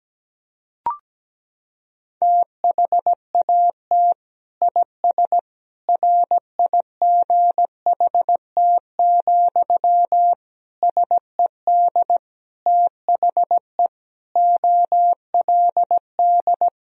Morse code, heard as a keyed tone, about 17 wpm, 700 Hz. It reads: THAT IS RIGHT, SED THE OLD
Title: Sentences from Top 200 Words and Abbreviations 17wpm